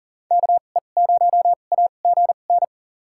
Morse code, heard as a keyed tone, about 40 wpm, spaced as for 29 wpm, 700 Hz. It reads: KE0AGN
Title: USA Call Signs - 6 Characters - 2x Character Spacing - 40wpm